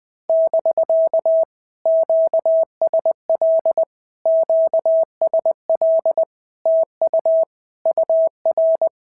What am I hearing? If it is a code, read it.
Morse code, heard as a keyed tone, about 19 wpm, 650 Hz. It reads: <BK> QSL QSL TU UR
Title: POTA QSO Practice - 20wpm